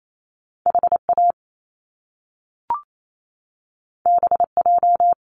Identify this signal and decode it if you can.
Morse code, heard as a keyed tone, about 28 wpm, 700 Hz. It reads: H A B J
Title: Top Two Letter Combinations in Words 28wpm